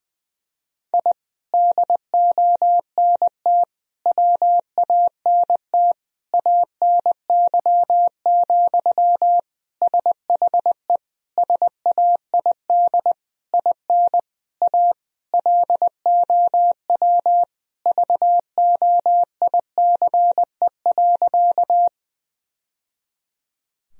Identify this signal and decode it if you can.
Morse code, heard as a keyed tone, about 20 wpm, 700 Hz. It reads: I DONT WANT ANY, SHE SAID IN A LOW VOICE.
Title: Sentences from Top 1000 Words - Review Part 1 - 20wpm